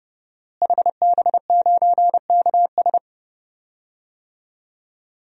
Morse code, heard as a keyed tone, about 31 wpm, 700 Hz. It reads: HB9KS
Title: DXCC-100 Call Signs - 5 Characters - Repeated 2x - 30wpm